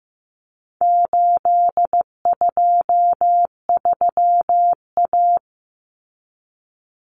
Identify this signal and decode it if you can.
Morse code, heard as a keyed tone, about 15 wpm, 700 Hz. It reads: 823A
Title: Common Cut Numbers - 4-Digit 15wpm